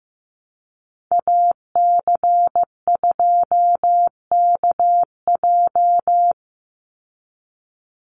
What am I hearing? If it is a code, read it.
Morse code, heard as a keyed tone, about 15 wpm, 700 Hz. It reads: AC2KJ
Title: Call Signs Repeated 2x - 15wpm